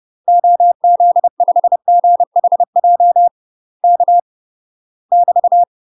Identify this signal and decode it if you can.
Morse code, heard as a keyed tone, about 30 wpm, 700 Hz. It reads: OZ5GHJ K <BT>
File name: UK MASTER QSOs TRAINING / 30WPM / UK MASTER QSOs 0031